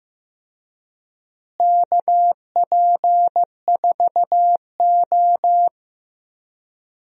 Morse code, heard as a keyed tone, about 15 wpm, 700 Hz. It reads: KP4O
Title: DXCC-100 Call Signs - 4 Characters - Repeated 3x - 15wpm